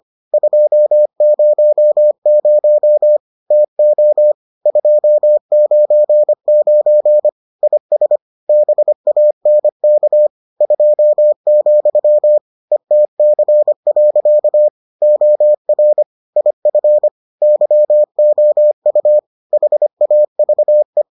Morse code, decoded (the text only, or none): 200 TO 299 IS BANK 2, ETC. OR IF YOU HAVE